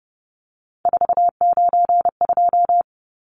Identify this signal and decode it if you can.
Morse code, heard as a keyed tone, about 30 wpm, 700 Hz. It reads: 492
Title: Numbers - 3-Digit 30wpm